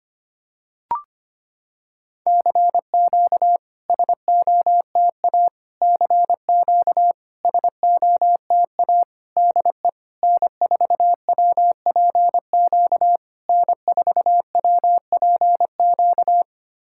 Morse code, heard as a keyed tone, about 25 wpm, 700 Hz. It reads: CQ SOTA CQ SOTA DE N4WPQ N4WPQ
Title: SOTA QSO Practice 25wpm